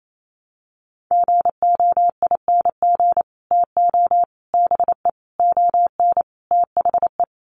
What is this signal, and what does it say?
Morse code, heard as a keyed tone, about 28 wpm, 700 Hz. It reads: GOING TO BE ON THE
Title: Sets of 5 Words Encore2 28wpm